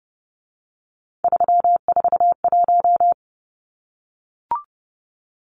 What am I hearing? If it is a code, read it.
Morse code, heard as a keyed tone, about 30 wpm, 700 Hz. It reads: 341
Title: Numbers - 3-Digit 30wpm